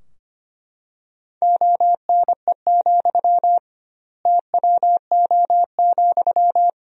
Morse code, heard as a keyed tone, about 25 wpm, 700 Hz. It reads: ONE, TWO,
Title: 2x Word-Spacing - Sentences from Top 200 Words - 25wpm